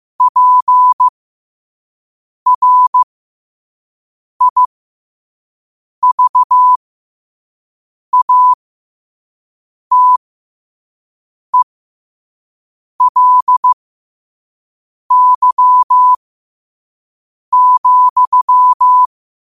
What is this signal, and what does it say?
Morse code, heard as a keyed tone, about 15 wpm, spaced as for 5 wpm, 1000 Hz. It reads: PRIVATELY,
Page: New quotes every day in morse code at 5 Words per minute.